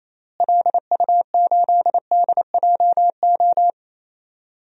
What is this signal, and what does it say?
Morse code, heard as a keyed tone, about 28 wpm, 700 Hz. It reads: LU8DJO
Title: DXCC-100 Call Signs - 6 Characters - Repeated 3x - 28wpm